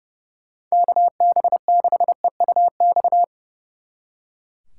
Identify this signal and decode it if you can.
Morse code, heard as a keyed tone, about 37 wpm, 700 Hz. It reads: KB6EUX
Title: Speed Racing - USA 2x3 Call Signs 30wpm